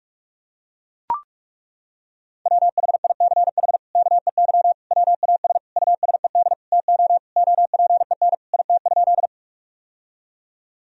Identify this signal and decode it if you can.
Morse code, heard as a keyed tone, about 41 wpm, 700 Hz. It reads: WHICH KEY WAS USED TO OPEN IT?
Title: Sentences from Top 1000 Words - Review Part 4 - 45wpm